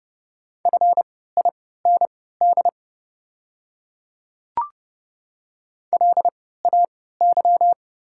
Morse code, heard as a keyed tone, about 31 wpm, spaced as for 17 wpm, 690 Hz. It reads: FIND LAY
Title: 3x Character Spacing - Top 500 Words - 30wpm